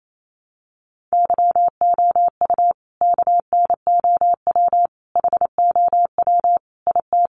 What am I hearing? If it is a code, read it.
Morse code, heard as a keyed tone, about 28 wpm, 700 Hz. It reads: YOU KNOW HOW IT
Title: Sentences from Top 100 Words 28wpm